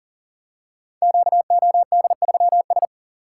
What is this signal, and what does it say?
Morse code, heard as a keyed tone, about 42 wpm, 700 Hz. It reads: QOD3S
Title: Code Groups - 5 Letter:Numbers - 40wpm